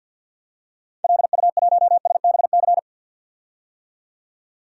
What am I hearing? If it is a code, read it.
Morse code, heard as a keyed tone, about 51 wpm, 700 Hz. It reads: LU1SBC
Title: DXCC-100 Call Signs - 6 Characters - 50wpm